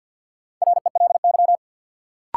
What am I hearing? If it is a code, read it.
Morse code, heard as a keyed tone, about 53 wpm, 700 Hz. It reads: RELY
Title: Sentences from Top 1200 Words - New Word Drill - 50wpm